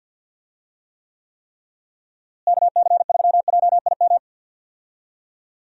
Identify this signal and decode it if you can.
Morse code, heard as a keyed tone, about 50 wpm, 700 Hz. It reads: KC3JIM
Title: USA Call Signs - 6 Characters - Repeated 3x - 50wpm